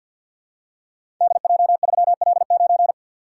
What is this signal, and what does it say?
Morse code, heard as a keyed tone, about 53 wpm, 700 Hz. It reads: DJ3L9
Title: Code Groups - 5 Letter:Numbers - 50wpm